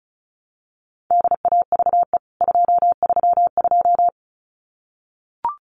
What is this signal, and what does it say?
Morse code, heard as a keyed tone, about 34 wpm, 700 Hz. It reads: DAVE 232
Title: CWT Exchange 35wpm